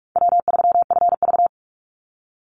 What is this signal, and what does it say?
Morse code, heard as a keyed tone, about 46 wpm, 700 Hz. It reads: W3FV
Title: Easy Call Signs - 45wpm